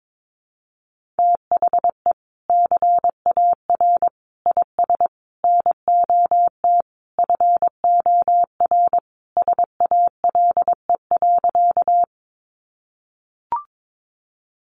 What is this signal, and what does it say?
Morse code, heard as a keyed tone, about 22 wpm, 700 Hz. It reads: THE CAR IS NOT FOR SALE.
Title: Sentences from Top 1400 Words 22wpm